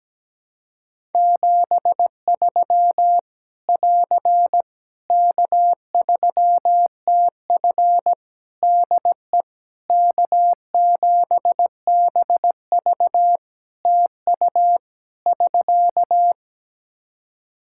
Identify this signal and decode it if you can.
Morse code, heard as a keyed tone, about 17 wpm, 700 Hz. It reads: 73 <AR> K3TF DE K7BV TU <SK>
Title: Practice QSO - Exchange 4 - Ending - 17wpm